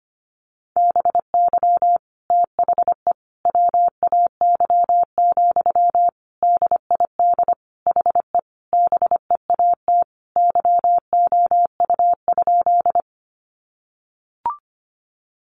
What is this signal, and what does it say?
Morse code, heard as a keyed tone, about 25 wpm, 700 Hz. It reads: BY THE WAY, DID HE BEAT YOU?
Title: Sentences from Top 1300 Words 25wpm